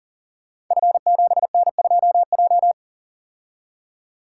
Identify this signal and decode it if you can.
Morse code, heard as a keyed tone, about 39 wpm, 700 Hz. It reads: F7N2J
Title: Code Groups - 5 Letter:Numbers - 40wpm